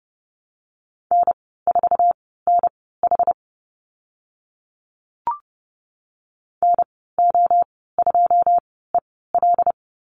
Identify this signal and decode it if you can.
Morse code, heard as a keyed tone, about 30 wpm, spaced as for 17 wpm, 700 Hz. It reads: N4NH NO2EL
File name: Lesson 15g - Number 5 - Call Signs 30wpm